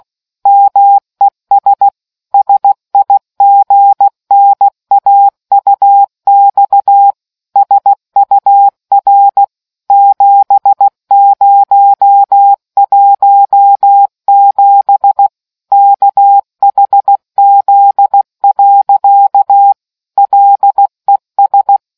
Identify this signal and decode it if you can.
Morse code, heard as a keyed tone, about 16 wpm, 800 Hz. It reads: MES SIGNAUX SUR 7017 KHZ. LES